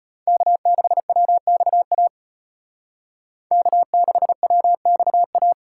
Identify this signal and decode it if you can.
Morse code, heard as a keyed tone, about 30 wpm, 700 Hz. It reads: K6WXA K6WXA
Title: Speed Racing - USA 1x3 Call Signs 25wpm